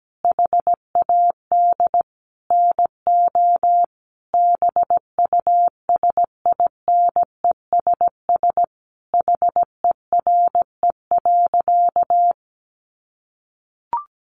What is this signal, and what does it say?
Morse code, heard as a keyed tone, about 17 wpm, 700 Hz. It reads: HAD NO BUSINESS HERE.
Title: Sentences from Top 1300 Words 17wpm